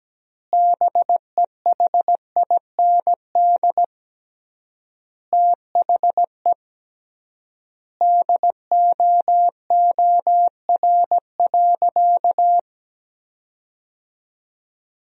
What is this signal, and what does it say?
Morse code, heard as a keyed tone, about 17 wpm, 700 Hz. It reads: BEHIND THE DOOR.
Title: 3x Word-Spacing - Sentences from Top 500 Words - 17wpm